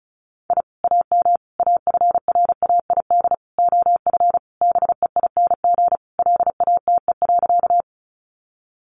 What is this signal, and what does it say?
Morse code, heard as a keyed tone, about 34 wpm, 700 Hz. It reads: I AM AFRAID OF BEING LATE.
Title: Sentences from Top 1500 Words 35wpm